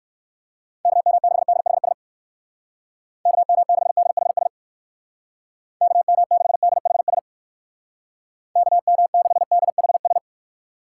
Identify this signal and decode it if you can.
Morse code, heard as a keyed tone, about 59 wpm, 700 Hz. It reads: KM6DHS KM6DHS KM6DHS KM6DHS